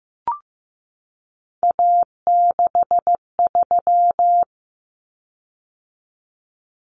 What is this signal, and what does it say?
Morse code, heard as a keyed tone, about 14 wpm, 700 Hz. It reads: A63
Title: Common Cut Numbers - 3-Digit 15wpm